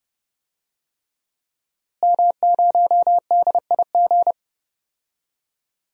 Morse code, heard as a keyed tone, about 32 wpm, 700 Hz. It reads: M0DIG
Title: DXCC-100 Call Signs - 5 Characters - Repeated 3x - 30wpm